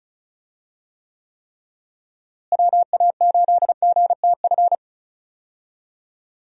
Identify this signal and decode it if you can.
Morse code, heard as a keyed tone, about 36 wpm, 700 Hz. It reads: WA8GTF
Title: USA Call Signs - 6 Characters - Repeated 3x - 35wpm